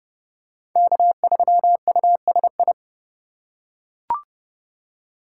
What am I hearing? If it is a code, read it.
Morse code, heard as a keyed tone, about 27 wpm, 700 Hz. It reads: K3USI